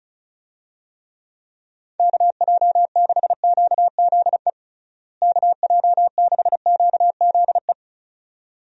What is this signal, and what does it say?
Morse code, heard as a keyed tone, about 34 wpm, 700 Hz. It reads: KJ6QZE KJ6QZE
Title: USA Call Signs - 6 Characters - Repeated 2x - 35wpm